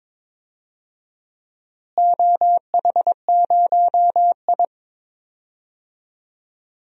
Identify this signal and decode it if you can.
Morse code, heard as a keyed tone, about 22 wpm, 700 Hz. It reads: OH0I
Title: DXCC-100 Call Signs - 4 Characters - Repeated 3x - 22wpm